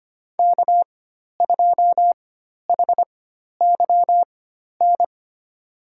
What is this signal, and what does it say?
Morse code, heard as a keyed tone, about 25 wpm, spaced as for 12 wpm, 700 Hz. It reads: K2HYN
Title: USA Call Signs - 5 Characters - 4x Character Spacing - 25wpm